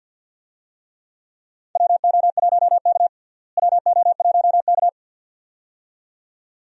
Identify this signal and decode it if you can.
Morse code, heard as a keyed tone, about 51 wpm, 700 Hz. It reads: WO1K WO1K
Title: USA Call Signs - 4 Characters - Repeated 2x - 50wpm